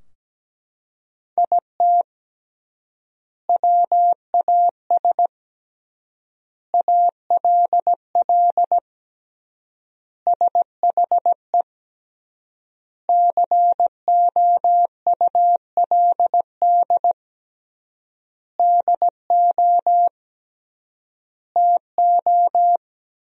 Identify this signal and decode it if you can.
Morse code, heard as a keyed tone, about 17 wpm, 700 Hz. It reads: IT WAS ALL SHE COULD DO TO
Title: 3x Word-Spacing - Sentences from Top 400 Words - 17wpm